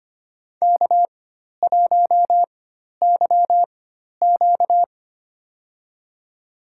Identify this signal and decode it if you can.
Morse code, heard as a keyed tone, about 25 wpm, spaced as for 12 wpm, 700 Hz. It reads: K1YQ
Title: USA Call Signs - 4 Characters - 4x Character Spacing - 25wpm